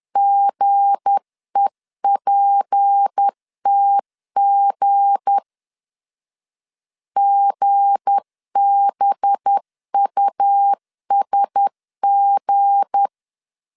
Morse code, heard as a keyed tone, about 10 wpm, 800 Hz. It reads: GEPTG GBUSG
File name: LEZIONI / 7 GF9Z / 7 G 10wpm 2501